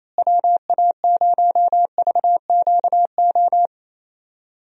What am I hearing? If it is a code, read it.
Morse code, heard as a keyed tone, about 28 wpm, 700 Hz. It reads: WA0VQO
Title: USA Call Signs - 6 Characters - Repeated 3x - 28wpm